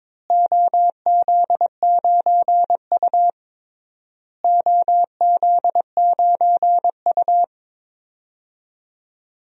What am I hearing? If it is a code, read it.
Morse code, heard as a keyed tone, about 22 wpm, 700 Hz. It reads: OZ9U OZ9U
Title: DXCC-100 Call Signs - 4 Characters - Repeated 3x - 22wpm